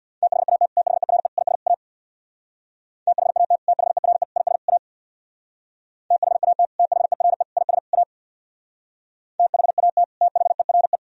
Morse code, heard as a keyed tone, about 64 wpm, 700 Hz. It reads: THAT THERE IS A THAT THERE IS A THAT THERE IS A THAT THERE